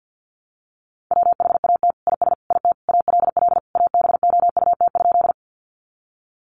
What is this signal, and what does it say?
Morse code, heard as a keyed tone, about 50 wpm, 700 Hz. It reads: WHAT IS IT ALL ABOUT?
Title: Sentences from Top 100 Words 50wpm